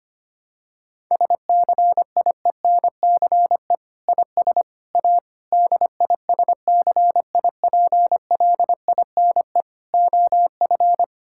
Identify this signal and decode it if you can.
Morse code, heard as a keyed tone, about 25 wpm, 700 Hz. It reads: SCIENCE IS A DISCIPLINE OF
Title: Sentences from Top 1000 Words - Review Part 6 - 25wpm